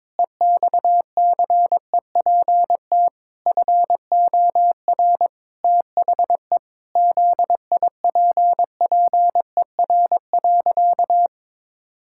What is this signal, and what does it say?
Morse code, heard as a keyed tone, about 22 wpm, 700 Hz. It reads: EXCEPT FOR THE ZIPPER.
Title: Pangram Challenge v2 - 22wpm